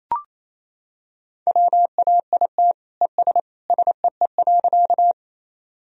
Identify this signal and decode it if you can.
Morse code, heard as a keyed tone, about 28 wpm, 700 Hz. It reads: WAIT ES SEE.
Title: Sentences from Top 500 Words and Abbreviations 28wpm